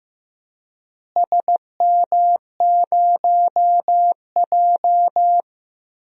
Morse code, heard as a keyed tone, about 15 wpm, 700 Hz. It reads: SM0J
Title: DXCC-100 Call Signs - 4 Characters - Repeated 3x - 15wpm